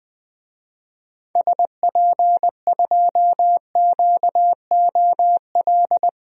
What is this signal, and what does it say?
Morse code, heard as a keyed tone, about 20 wpm, 700 Hz. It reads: SP2QOL
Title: DXCC-100 Call Signs - 6 Characters - 20wpm